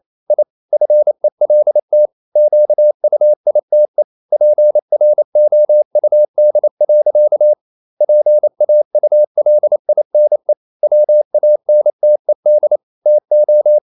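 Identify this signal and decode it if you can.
Morse code, heard as a keyed tone, about 28 wpm, 600 Hz. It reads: I FELT QUITE PROUD. PAULINE WANTED TO
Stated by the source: Morse code practice at 28 wpm, text of Ken Follett's novel Never in English, from CW med Gnister.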